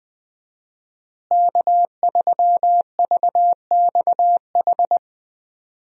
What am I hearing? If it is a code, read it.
Morse code, heard as a keyed tone, about 20 wpm, 700 Hz. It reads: K3VXH
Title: USA Call Signs - 5 Characters - Repeated 2x - 20wpm